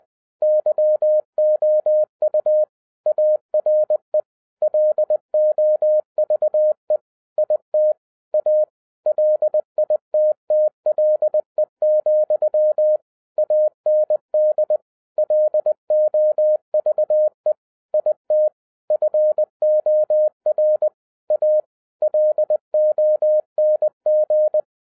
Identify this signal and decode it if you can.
Morse code, heard as a keyed tone, about 20 wpm, 610 Hz. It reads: YOU ARE LOVE IT A LITTLE, AND LOVE IT FOR A LONG
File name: 2011-10-mp3 / n13-TIPS FROM MY 34 YEARS OF EXPERIENCE 20wpm